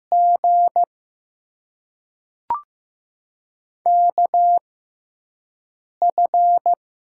Morse code, heard as a keyed tone, about 15 wpm, spaced as for 5 wpm, 700 Hz. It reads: G KF
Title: USA Call Signs - 6 Characters - 6x Character Spacing - 15wpm